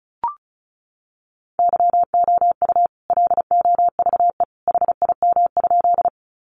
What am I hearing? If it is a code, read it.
Morse code, heard as a keyed tone, about 34 wpm, 700 Hz. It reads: YOU LOVE HIM?
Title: Sentences from Top 400 Words 35wpm